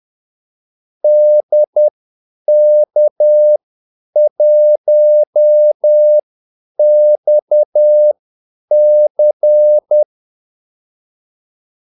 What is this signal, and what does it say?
Morse code, heard as a keyed tone, about 10 wpm, spaced as for 8 wpm, 600 Hz. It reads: DK1XC
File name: Callsigns-10-8wpm